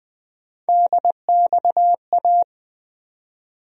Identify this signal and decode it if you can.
Morse code, heard as a keyed tone, about 20 wpm, 700 Hz. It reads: DXA
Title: USA Call Sign Postfixes - 3 Letters 20wpm